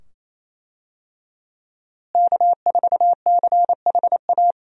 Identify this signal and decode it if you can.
Morse code, heard as a keyed tone, about 28 wpm, 700 Hz. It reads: K4CHA
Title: USA Call Signs - 5 Characters - Repeated 2x - 28wpm